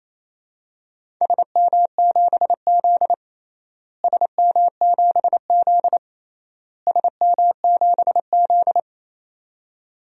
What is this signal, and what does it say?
Morse code, heard as a keyed tone, about 28 wpm, 700 Hz. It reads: SM7Z SM7Z SM7Z
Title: DXCC-100 Call Signs - 4 Characters - Repeated 3x - 28wpm